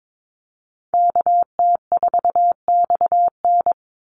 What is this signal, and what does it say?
Morse code, heard as a keyed tone, about 22 wpm, 700 Hz. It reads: KT4XN
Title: Call Signs Repeated 3x - 22wpm